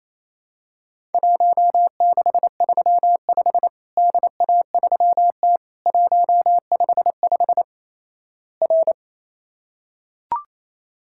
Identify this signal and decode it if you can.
Morse code, heard as a keyed tone, about 28 wpm, 700 Hz. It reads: 1635 DA3T 155 R
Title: Worked All Europe DX Contest - QTC Exchange - 28wpm